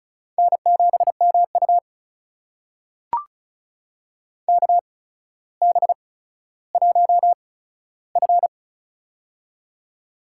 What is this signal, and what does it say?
Morse code, heard as a keyed tone, about 36 wpm, spaced as for 15 wpm, 700 Hz. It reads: N7MU KB1F
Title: USA Call Signs - 8x Character Spacing - 35wpm